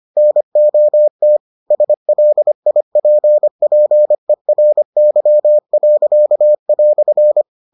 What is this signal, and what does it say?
Morse code, heard as a keyed tone, about 25 wpm, 600 Hz. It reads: NOT SLIPPERY."